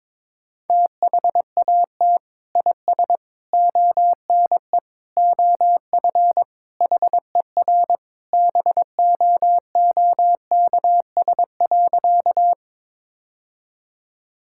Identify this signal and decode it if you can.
Morse code, heard as a keyed tone, about 22 wpm, 700 Hz. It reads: THAT IS ONE OF HER BOOKS.
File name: practice / 22wpm / Sentences with 2-Word Phrases - 22wpm